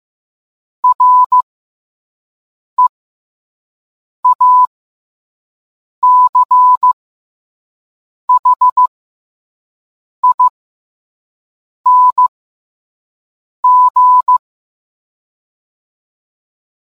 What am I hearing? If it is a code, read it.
Morse code, heard as a keyed tone, about 15 wpm, spaced as for 5 wpm, 1000 Hz. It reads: REACHING